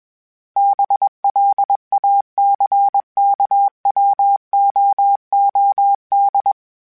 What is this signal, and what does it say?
Morse code, heard as a keyed tone, about 21 wpm, 800 Hz. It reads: BLACKWOOD
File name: fromSMs / FromSMs mp3s / Cities@20